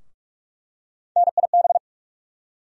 Morse code, heard as a keyed tone, about 43 wpm, 700 Hz. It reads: NIB
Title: USA Call Sign Postfixes - 3 Letters 45wpm